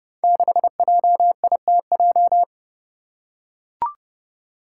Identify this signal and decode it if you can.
Morse code, heard as a keyed tone, about 32 wpm, 700 Hz. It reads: 6JITJ
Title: Five Challenging Letter-Number Combinations 30wpm